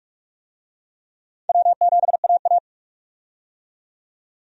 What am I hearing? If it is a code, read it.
Morse code, heard as a keyed tone, about 46 wpm, 700 Hz. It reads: W7AA
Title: USA Call Signs - 4 Characters - 45wpm